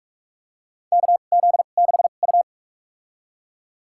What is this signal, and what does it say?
Morse code, heard as a keyed tone, about 47 wpm, spaced as for 33 wpm, 700 Hz. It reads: KZ6U
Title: USA Call Signs - 4 Characters - 2x Character Spacing - 45wpm